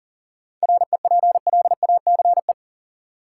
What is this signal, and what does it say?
Morse code, heard as a keyed tone, about 40 wpm, 700 Hz. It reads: REPLACE
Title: Sentences from Top 1400 Words - New Word Drill Alt - 40wpm